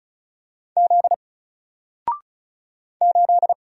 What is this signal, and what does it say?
Morse code, heard as a keyed tone, about 36 wpm, 700 Hz. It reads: Z 8
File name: Single-Letter-Number-35wpm